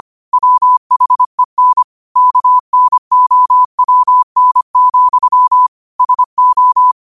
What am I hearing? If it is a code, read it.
Morse code, heard as a keyed tone, about 25 wpm, 1000 Hz. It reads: WHEN KNOWN, SO